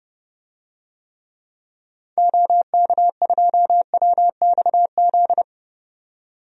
Morse code, heard as a keyed tone, about 32 wpm, 700 Hz. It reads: OK2WXZ
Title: DXCC-100 Call Signs - 6 Characters - 30wpm